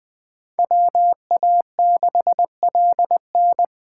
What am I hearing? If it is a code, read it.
Morse code, heard as a keyed tone, about 24 wpm, 700 Hz. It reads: WA6LN
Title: Speed Racing - USA Call Signs 20wpm